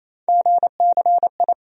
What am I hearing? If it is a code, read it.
Morse code, heard as a keyed tone, about 28 wpm, 700 Hz. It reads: GCI
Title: USA Call Sign Postfixes - 3 Letters 28wpm